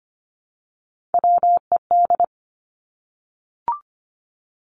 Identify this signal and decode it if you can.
Morse code, heard as a keyed tone, about 25 wpm, 700 Hz. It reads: WED
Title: Three Challenging Letter-Number Combinations 25wpm